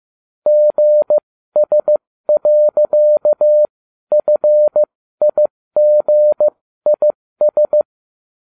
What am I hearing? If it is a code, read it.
Morse code, heard as a keyed tone, about 15 wpm, spaced as for 13 wpm, 600 Hz. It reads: GS.FIGIS